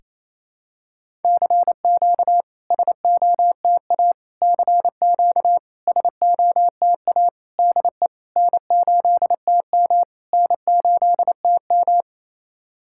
Morse code, heard as a keyed tone, about 28 wpm, 690 Hz. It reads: CQ SOTA CQ SOTA DE N8TM N8TM
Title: SOTA QSO Practice 28wpm